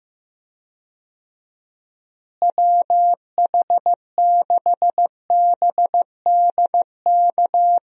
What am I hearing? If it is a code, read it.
Morse code, heard as a keyed tone, about 15 wpm, 700 Hz. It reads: WH6BDK
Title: USA Call Signs - 6 Characters - Repeated 2x - 15wpm